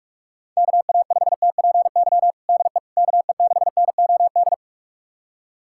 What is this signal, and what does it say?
Morse code, heard as a keyed tone, about 46 wpm, 700 Hz. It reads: KA5TPY DE KE6NOD
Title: USA Callsign DE Callsign - 6 Characters - 45wpm